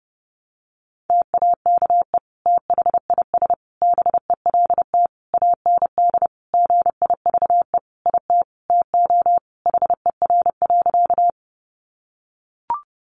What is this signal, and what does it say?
Morse code, heard as a keyed tone, about 30 wpm, 700 Hz. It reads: TAKE THIS BELT AND GIVE IT TO HER.
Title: Sentences from Top 1400 Words 30wpm